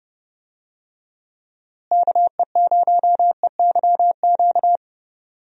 Morse code, heard as a keyed tone, about 30 wpm, 700 Hz. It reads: KE0EYQ
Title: USA Call Signs - 6 Characters - Repeated 2x - 30wpm